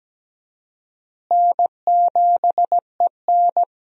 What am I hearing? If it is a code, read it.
Morse code, heard as a keyed tone, about 17 wpm, 700 Hz. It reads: N7EN
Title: USA Call Signs - 4 Characters - Repeated 3x - 17wpm